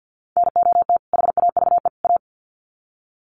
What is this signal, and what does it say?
Morse code, heard as a keyed tone, about 49 wpm, 700 Hz. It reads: NOT HAVE A
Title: Sets of 3 Words 50wpm